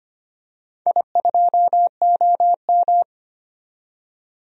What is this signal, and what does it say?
Morse code, heard as a keyed tone, about 25 wpm, 700 Hz. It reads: I2OM
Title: DXCC-100 Call Signs - 4 Characters - 25wpm